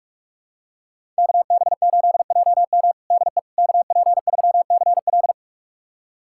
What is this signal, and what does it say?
Morse code, heard as a keyed tone, about 46 wpm, 700 Hz. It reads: KB8JM DE KP3CL
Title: USA Callsign DE Callsign - 5 Characters - 45wpm